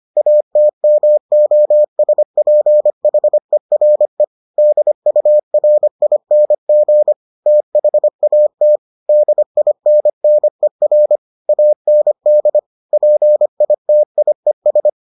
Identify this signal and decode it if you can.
Morse code, heard as a keyed tone, about 25 wpm, 600 Hz. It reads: ATMOSPHERE DURING THAT DINNER AND PITIES